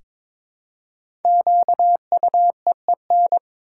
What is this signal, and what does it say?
Morse code, heard as a keyed tone, about 22 wpm, 690 Hz. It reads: QUEEN
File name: 5 Letter Words - First 3 Letters Given - 22wpm